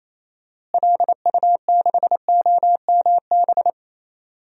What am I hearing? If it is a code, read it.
Morse code, heard as a keyed tone, about 28 wpm, 700 Hz. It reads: LU6OMB
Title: DXCC-100 Call Signs - 6 Characters - Repeated 3x - 28wpm